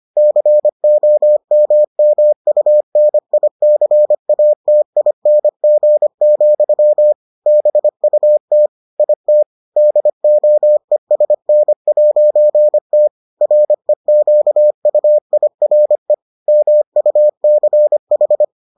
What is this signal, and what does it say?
Morse code, heard as a keyed tone, about 25 wpm, 600 Hz. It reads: COMMUNICATING, BUT IT DOESN'T REQUIRE MUCH